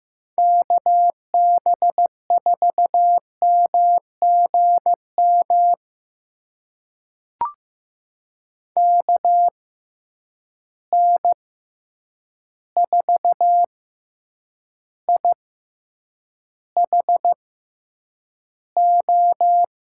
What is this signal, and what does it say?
Morse code, heard as a keyed tone, about 15 wpm, spaced as for 8 wpm, 700 Hz. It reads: KB4MGM KN4IHO
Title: USA Call Signs - 6 Characters - 6x Character Spacing - 15wpm